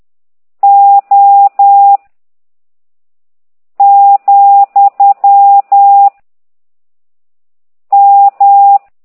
Morse code, heard as a keyed tone, about 10 wpm, 800 Hz. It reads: O , M
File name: ct2idw / exerc15